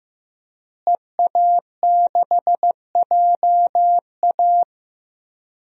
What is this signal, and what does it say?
Morse code, heard as a keyed tone, about 15 wpm, 700 Hz. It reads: EA6JA
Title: DXCC-100 Call Signs - 5 Characters - Repeated 2x - 15wpm